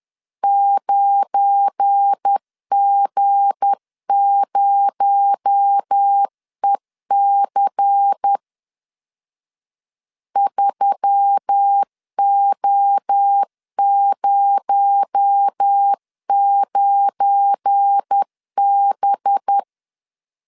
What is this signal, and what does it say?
Morse code, heard as a keyed tone, about 10 wpm, 800 Hz. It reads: 9G0EC 3O09B
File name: LEZIONI / 7 GF9Z / 7 GF9 10wpm 2747